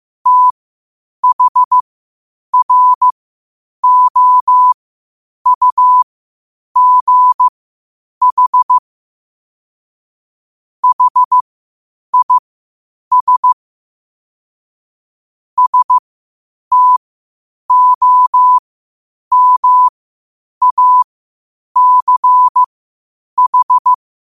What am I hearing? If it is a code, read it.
Morse code, heard as a keyed tone, about 15 wpm, spaced as for 8 wpm, 1000 Hz. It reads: THROUGH HIS STOMACH
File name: qotdfiles / 8wpm